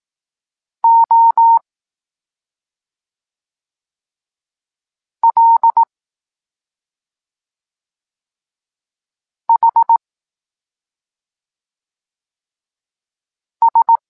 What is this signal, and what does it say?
Morse code, heard as a keyed tone, about 18 wpm, 920 Hz. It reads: O L H S
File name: Tekst-26 3